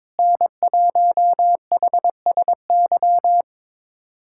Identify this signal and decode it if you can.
Morse code, heard as a keyed tone, about 22 wpm, 700 Hz. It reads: N1HSY
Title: USA Callsign DE Callsign - 5 Characters - 22wpm